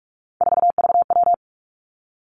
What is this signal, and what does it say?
Morse code, heard as a keyed tone, about 46 wpm, 700 Hz. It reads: 4VW
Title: Three Challenging Letter-Number Combinations 45wpm